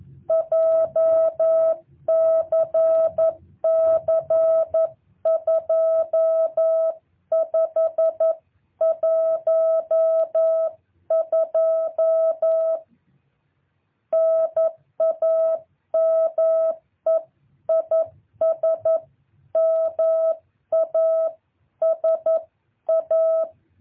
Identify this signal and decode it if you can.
Morse code, heard as a keyed tone, about 11 wpm, 650 Hz. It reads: JCC 2512 NAME IS MASA
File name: jh8cbh / 2023cw